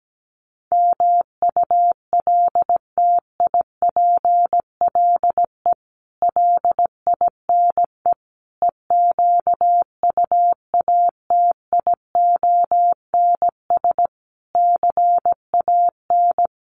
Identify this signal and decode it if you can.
Morse code, heard as a keyed tone, about 17 wpm, 700 Hz. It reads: MULTIPLE LINE EQUATIONS CAN
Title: Sets of 5 Words 17wpm